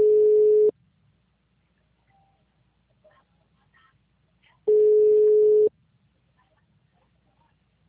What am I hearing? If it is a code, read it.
Ringback tone (Europe, 425 Hz).